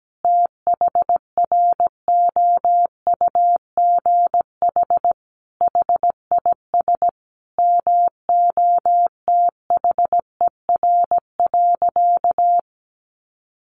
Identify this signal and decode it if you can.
Morse code, heard as a keyed tone, about 17 wpm, 700 Hz. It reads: THROUGH HIS MOTHER.
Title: Sentences from Top 1200 Words 17wpm